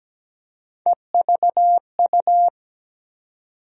Morse code, heard as a keyed tone, about 17 wpm, 700 Hz. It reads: EVU
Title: USA Call Sign Postfixes - 3 Letters 17wpm